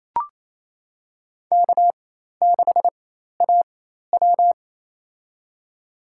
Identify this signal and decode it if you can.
Morse code, heard as a keyed tone, about 28 wpm, spaced as for 13 wpm, 700 Hz. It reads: K6AW
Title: USA Call Signs - 4 Characters - 4x Character Spacing - 28wpm